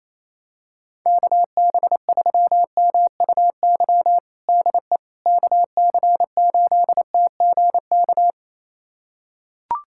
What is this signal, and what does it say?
Morse code, heard as a keyed tone, about 28 wpm, 700 Hz. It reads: KB3MUY DE KC8TGK
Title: USA Callsign DE Callsign - 6 Characters - 28wpm